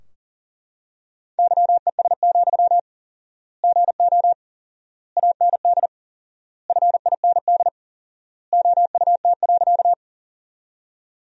Morse code, heard as a keyed tone, about 40 wpm, 700 Hz. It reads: YES, GO AND FIND OUT.
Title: 4x Word-Spacing - Sentences from Top 500 Words - 40wpm